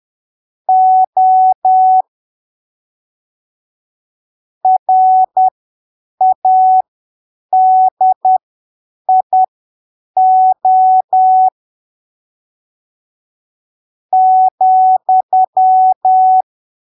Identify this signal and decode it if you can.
Morse code, heard as a keyed tone, about 10 wpm, spaced as for 7 wpm, 750 Hz. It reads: O RADIO ,
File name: arquivos / texto / texto 14 10ppm